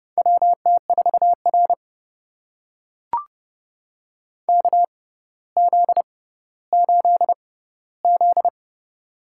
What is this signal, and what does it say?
Morse code, heard as a keyed tone, about 31 wpm, spaced as for 15 wpm, 700 Hz. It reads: WT4R KZ8Z
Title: USA Call Signs - 4 Characters - 6x Character Spacing - 30wpm